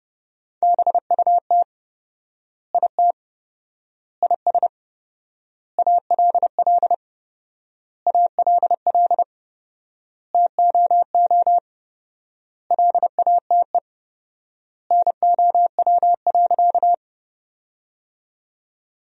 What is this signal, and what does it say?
Morse code, heard as a keyed tone, about 29 wpm, 700 Hz. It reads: BUT IT IS ALL ALL TOO LATE NOW.
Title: 4x Word-Spacing - Sentences from Top 300 Words - 30wpm